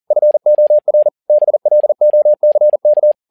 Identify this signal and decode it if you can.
Morse code, heard as a keyed tone, about 40 wpm, 600 Hz. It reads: FOR BLOCK
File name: learning center / tower trouble / tower touble 40wpm